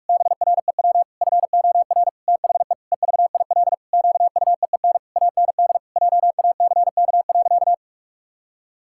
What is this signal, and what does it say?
Morse code, heard as a keyed tone, about 46 wpm, 700 Hz. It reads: BREW FOR THE EVIL QUEEN AND JACK.
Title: Pangram Challenge v2 - 45wpm